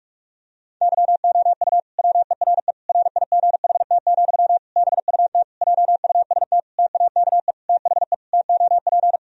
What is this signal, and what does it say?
Morse code, heard as a keyed tone, about 46 wpm, 700 Hz. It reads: YOU WERE RIGHT, BUT JUST TAKE THE TOP
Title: Sentences from Top 1000 Words - Review Part 2 - 45wpm